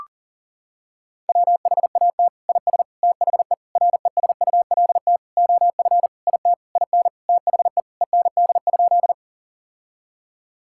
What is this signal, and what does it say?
Morse code, heard as a keyed tone, about 40 wpm, 700 Hz. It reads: WHAT IS THE RESULT OF IT IN THE END?
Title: Sentences from Top 1000 Words - Review Part 5 - 40wpm